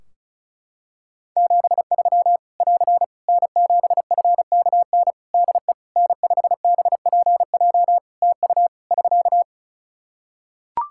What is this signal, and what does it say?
Morse code, heard as a keyed tone, about 34 wpm, 700 Hz. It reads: 73 <AR> N7FKN DE N5BPJ TU <SK>
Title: Practice QSO - Exchange 4 - Ending - 35wpm